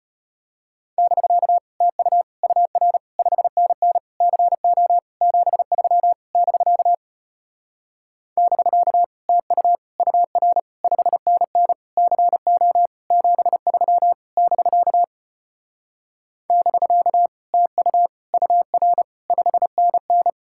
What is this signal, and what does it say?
Morse code, heard as a keyed tone, about 33 wpm, 700 Hz. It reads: <BK> TU UR 5NN CO 73 <BK> <BK> TU UR 5NN CO 73 <BK> <BK> TU UR 5NN